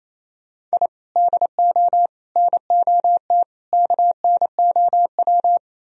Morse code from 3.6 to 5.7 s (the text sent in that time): KNOW